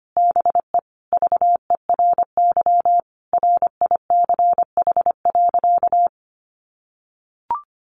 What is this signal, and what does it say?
Morse code, heard as a keyed tone, about 25 wpm, 700 Hz. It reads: BE VERY RICH.
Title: Sentences from Top 1200 Words 25wpm